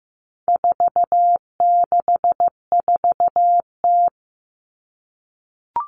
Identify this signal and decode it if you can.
Morse code, heard as a keyed tone, about 15 wpm, 700 Hz. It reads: 464T
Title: All Cut Numbers - 4-Digit 15wpm